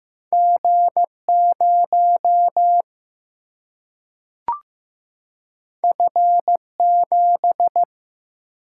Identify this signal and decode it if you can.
Morse code, heard as a keyed tone, about 15 wpm, 700 Hz. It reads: G 0 F 7
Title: Single-Letter Single-Number 15wpm